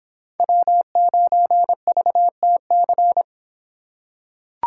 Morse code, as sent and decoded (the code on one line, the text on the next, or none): .-- ----. ...- - -.-.
W9VTC